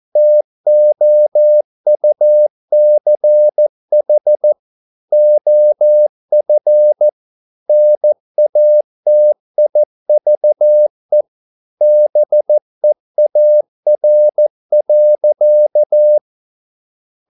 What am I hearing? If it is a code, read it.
Morse code, heard as a keyed tone, about 14 wpm, 600 Hz. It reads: TOUCH OF NATIVE BEAR.
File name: mp3 files / a dogs mistake - 14wpm